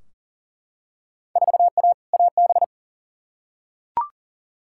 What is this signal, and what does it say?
Morse code, heard as a keyed tone, about 40 wpm, 700 Hz. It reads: 4A AB
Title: Field Day Exchanges Encore - 40wpm